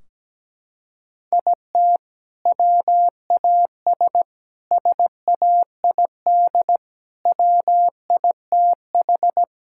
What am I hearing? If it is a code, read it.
Morse code, heard as a keyed tone, about 17 wpm, 700 Hz. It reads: IT WAS SAID WITH
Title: Sentences from Top 1600 Words 17wpm